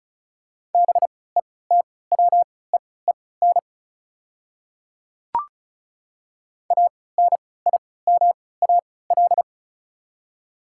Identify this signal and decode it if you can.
Morse code, heard as a keyed tone, about 34 wpm, spaced as for 20 wpm, 700 Hz. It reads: BETWEEN ANIMAL
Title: 3x Character Spacing - Top 300 Words - 35wpm